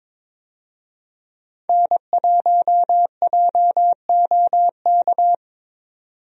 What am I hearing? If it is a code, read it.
Morse code, heard as a keyed tone, about 23 wpm, 700 Hz. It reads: N1JOK
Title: USA Call Signs - 5 Characters - 22wpm